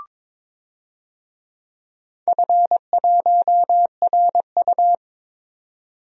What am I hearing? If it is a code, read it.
Morse code, heard as a keyed tone, about 23 wpm, 700 Hz. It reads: F1RU
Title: DXCC-100 Call Signs - 4 Characters - 22wpm